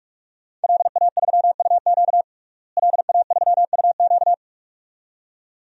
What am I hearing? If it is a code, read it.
Morse code, heard as a keyed tone, about 46 wpm, 700 Hz. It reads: LA3UQ LA3UQ
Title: DXCC-100 Call Signs - 5 Characters - Repeated 2x - 45wpm